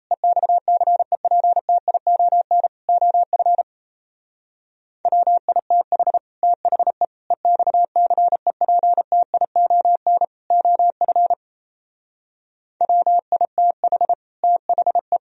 Morse code, heard as a keyed tone, about 34 wpm, 700 Hz. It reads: EXCEPTION OF WITH THE EXCEPTION OF WITH THE
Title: Speed-Racing - Sets of 4 Words 28wpm